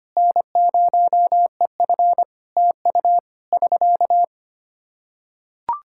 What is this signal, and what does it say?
Morse code, heard as a keyed tone, about 25 wpm, 700 Hz. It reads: N0EF TU <SK>
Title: Practice QSO - Exchange 4 - Ending - 25wpm